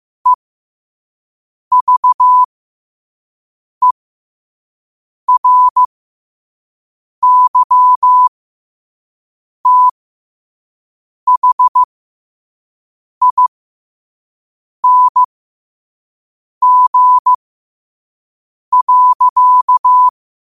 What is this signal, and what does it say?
Morse code, heard as a keyed tone, about 15 wpm, spaced as for 5 wpm, 1000 Hz. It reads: EVERYTHING.
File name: qotdfiles / 5wpm